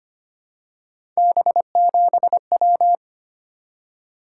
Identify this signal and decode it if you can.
Morse code, heard as a keyed tone, about 25 wpm, 700 Hz. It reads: B7W
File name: Three Letter-Number Combinations - 25wpm